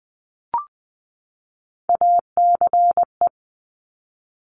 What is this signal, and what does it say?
Morse code, heard as a keyed tone, about 19 wpm, 700 Hz. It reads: ACE
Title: Top Three Letter Combinations in Words 20wpm